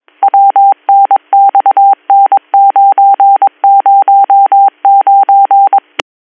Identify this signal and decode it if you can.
Morse code, heard as a keyed tone, about 22 wpm, 790 Hz. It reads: WNXN909
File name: 202- .0-call 17072